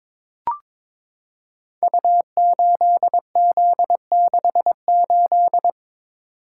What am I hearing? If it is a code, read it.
Morse code, heard as a keyed tone, about 22 wpm, 690 Hz. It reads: U8Z68
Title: Code Groups - 5 Letter:Numbers - 22wpm